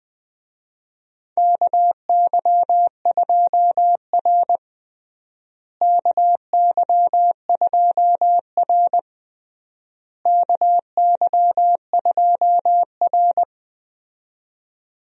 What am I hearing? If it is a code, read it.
Morse code, heard as a keyed tone, about 20 wpm, 700 Hz. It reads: KY2R KY2R KY2R
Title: USA Call Signs - 4 Characters - Repeated 3x - 20wpm